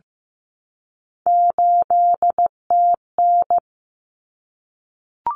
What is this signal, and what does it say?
Morse code, heard as a keyed tone, about 15 wpm, 700 Hz. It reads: 8TN
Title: Common Cut Numbers - 3-Digit 15wpm